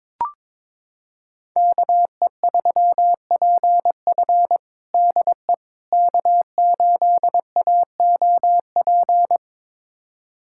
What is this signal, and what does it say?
Morse code, heard as a keyed tone, about 22 wpm, 700 Hz. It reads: KE3PF DE K8AOP
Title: USA Callsign DE Callsign - 5 Characters - 22wpm